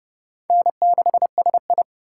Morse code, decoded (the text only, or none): N6SI